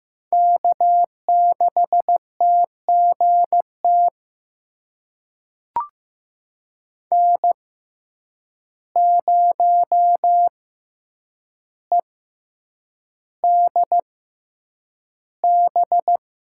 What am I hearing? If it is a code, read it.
Morse code, heard as a keyed tone, about 16 wpm, spaced as for 8 wpm, 700 Hz. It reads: K6TGT N0EDB
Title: USA Call Signs - 5 Characters - 6x Character Spacing - 15wpm